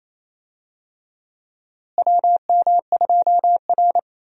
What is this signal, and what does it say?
Morse code, heard as a keyed tone, about 28 wpm, 700 Hz. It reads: WM2R
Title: USA Call Signs - 4 Characters - Repeated 3x - 28wpm